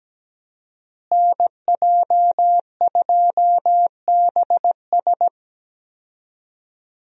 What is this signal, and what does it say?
Morse code, heard as a keyed tone, about 17 wpm, 700 Hz. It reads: NJ2BS
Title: USA Call Signs - 5 Characters - Repeated 3x - 17wpm